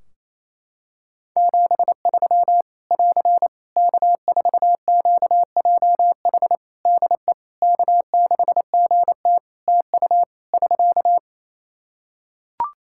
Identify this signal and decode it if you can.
Morse code, heard as a keyed tone, about 28 wpm, 700 Hz. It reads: 73 <AR> K4QJH DE K6GT TU <SK>
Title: Practice QSO - Exchange 4 - Ending - 28wpm